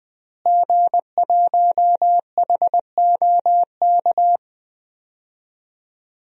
Morse code, heard as a keyed tone, about 21 wpm, 700 Hz. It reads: G1HOK
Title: DXCC-100 Call Signs - 5 Characters - Repeated 3x - 20wpm